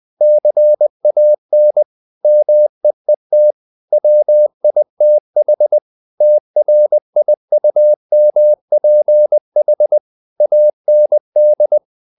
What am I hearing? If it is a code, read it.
Morse code, heard as a keyed tone, about 20 wpm, 600 Hz. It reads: CAN MEET WITH TRIUMPH AND
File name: CW / Practice / If20